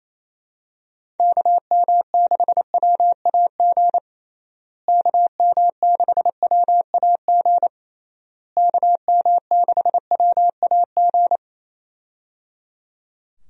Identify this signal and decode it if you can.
Morse code, heard as a keyed tone, about 27 wpm, 700 Hz. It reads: KM6WAG KM6WAG KM6WAG
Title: USA Call Signs - 6 Characters - Repeated 3x - 28wpm